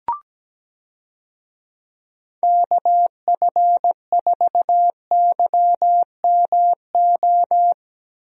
Morse code, heard as a keyed tone, about 17 wpm, 700 Hz. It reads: KF4YMO
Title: USA Call Signs - 6 Characters - Repeated 3x - 17wpm